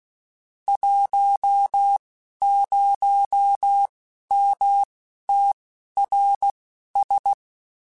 Morse code, heard as a keyed tone, about 16 wpm, spaced as for 11 wpm, 800 Hz. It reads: 10MTRS
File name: (spaced) / QSOs Spaced-mp3s / WD0FBT 1 @15 sp